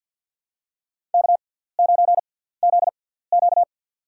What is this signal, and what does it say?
Morse code, heard as a keyed tone, about 52 wpm, spaced as for 17 wpm, 700 Hz. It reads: K9ZQ
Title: USA Call Signs - 6x Character Spacing - 50wpm